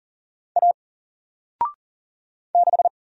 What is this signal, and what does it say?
Morse code, heard as a keyed tone, about 40 wpm, 700 Hz. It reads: A 6
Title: Single Letter-Number Alt 40wpm